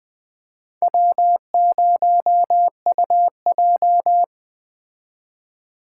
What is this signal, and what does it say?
Morse code, heard as a keyed tone, about 20 wpm, 700 Hz. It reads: W0UJ
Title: Speed Racing - USA Call Signs 17wpm